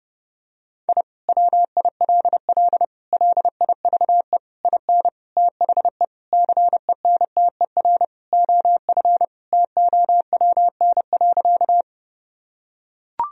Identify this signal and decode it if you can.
Morse code, heard as a keyed tone, about 30 wpm, 700 Hz. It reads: I WILL LIVE IN THE CENTER OF TOWN.
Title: Sentences from Top 1000 Words - Review Part 2 - 30wpm